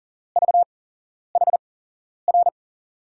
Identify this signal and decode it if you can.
Morse code, heard as a keyed tone, about 42 wpm, spaced as for 11 wpm, 700 Hz. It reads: VHR
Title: USA Call Signs - 6 Characters - 8x Character Spacing - 40wpm